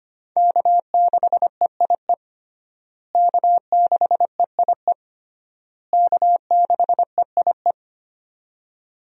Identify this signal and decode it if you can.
Morse code, heard as a keyed tone, about 25 wpm, 700 Hz. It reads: K6EIE K6EIE K6EIE
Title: USA Call Signs - 5 Characters - Repeated 3x - 25wpm